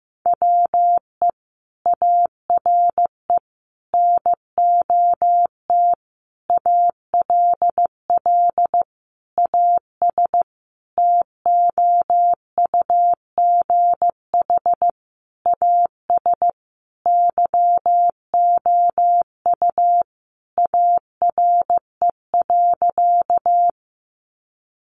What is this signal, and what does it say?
Morse code, heard as a keyed tone, about 15 wpm, 700 Hz. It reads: WE ARE NOT ALL AS TOUGH AS YOU ARE.
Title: Sentences from Top 1200 Words 15wpm